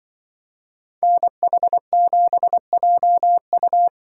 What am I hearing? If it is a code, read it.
Morse code, heard as a keyed tone, about 23 wpm, 700 Hz. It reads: NH7JU
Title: Speed Racing - USA Call Signs 20wpm